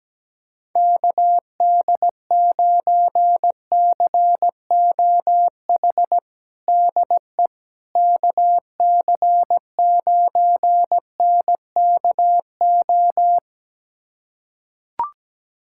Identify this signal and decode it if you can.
Morse code, heard as a keyed tone, about 17 wpm, 700 Hz. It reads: KD9COH DE KC9NKO
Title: USA Callsign DE Callsign - 6 Characters - 17wpm